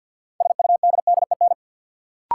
Morse code, heard as a keyed tone, about 52 wpm, 700 Hz. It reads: SUDDEN
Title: Sets of 4 Words - New Word Drill - 50wpm